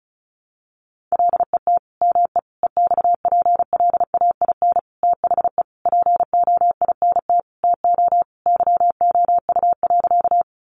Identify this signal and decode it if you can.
Morse code, heard as a keyed tone, about 34 wpm, 700 Hz. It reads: LET ME EXPLAIN THE POINT TO YOU.
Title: Sentences from Top 1100 Words 35wpm